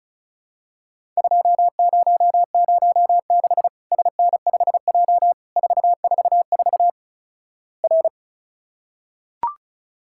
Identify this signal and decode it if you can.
Morse code, heard as a keyed tone, about 36 wpm, 700 Hz. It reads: 2006 SN5J 444 R
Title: Worked All Europe DX Contest - QTC Exchange - 35wpm